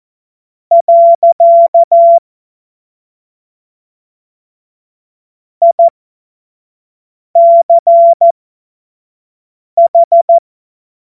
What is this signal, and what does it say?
Morse code, heard as a keyed tone, about 14 wpm, spaced as for 5 wpm, 680 Hz. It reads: . ICH
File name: Morsekurs / Witze / Witz 01. Februar 2022 25 F70